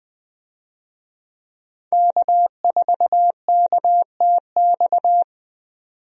Morse code, heard as a keyed tone, about 20 wpm, 700 Hz. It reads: K4KTX
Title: USA Call Signs - 5 Characters - 20wpm